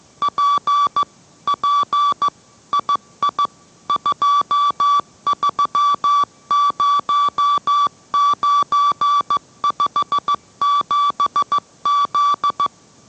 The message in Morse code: P P II 230957Z

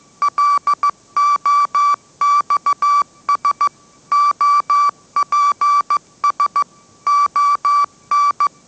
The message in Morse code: LOXS OPS ON

